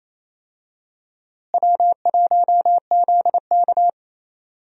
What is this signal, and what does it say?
Morse code, heard as a keyed tone, about 28 wpm, 700 Hz. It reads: W1ZK
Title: USA Call Signs - 4 Characters - Repeated 3x - 28wpm